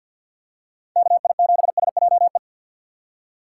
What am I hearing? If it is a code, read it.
Morse code, heard as a keyed tone, about 48 wpm, 700 Hz. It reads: KI7SJE
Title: USA Call Signs - 6 Characters - 4x Character Spacing - 50wpm